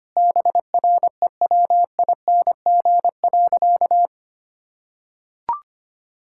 Morse code, heard as a keyed tone, about 25 wpm, 700 Hz. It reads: BREWING.
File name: Easy Sentences with Hints - 25wpm